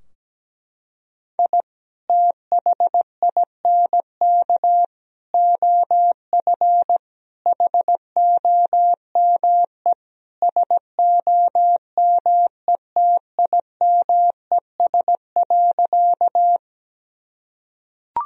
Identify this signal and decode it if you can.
Morse code, heard as a keyed tone, about 17 wpm, 700 Hz. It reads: I THINK OF HOME SOMETIMES.
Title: Sentences from Top 1000 Words - Review Part 3 - 17wpm